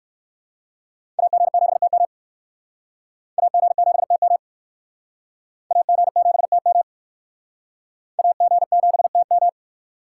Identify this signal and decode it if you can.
Morse code, heard as a keyed tone, about 56 wpm, 700 Hz. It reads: AG7TM AG7TM AG7TM AG7TM
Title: Speed Racing - USA 2x2 Call Signs 45wpm